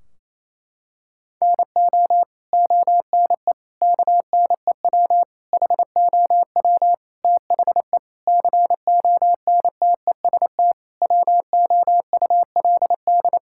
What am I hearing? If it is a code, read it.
Morse code, heard as a keyed tone, about 28 wpm, 700 Hz. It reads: NO ONE KNEW HOW THE CONTEST WOULD
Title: Sentences from Top 1600 Words 28wpm